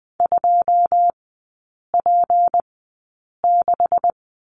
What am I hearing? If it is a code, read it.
Morse code, heard as a keyed tone, about 20 wpm, spaced as for 8 wpm, 700 Hz. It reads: 2P6
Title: 152 Lesson 25e - Letter p - Three Characters 20wpm